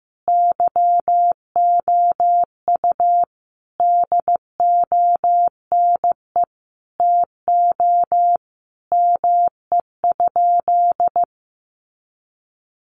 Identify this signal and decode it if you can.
Morse code, heard as a keyed tone, about 14 wpm, 700 Hz. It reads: YOU DONE TO ME?
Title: Sentences from Top 500 Words 15wpm